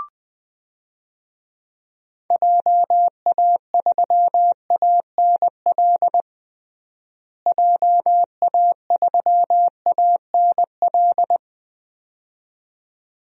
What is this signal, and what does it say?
Morse code, heard as a keyed tone, about 20 wpm, 700 Hz. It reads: JA3ANL JA3ANL
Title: DXCC-100 Call Signs - 6 Characters - Repeated 2x - 20wpm